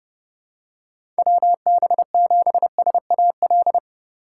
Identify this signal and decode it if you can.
Morse code, heard as a keyed tone, about 30 wpm, 700 Hz. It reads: WB7SAL
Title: USA Call Signs - 6 Characters - 4x Character Spacing - 30wpm